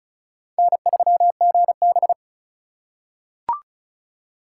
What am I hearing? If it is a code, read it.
Morse code, heard as a keyed tone, about 36 wpm, 700 Hz. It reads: N3GB
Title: USA Call Signs - 4 Characters - 4x Character Spacing - 35wpm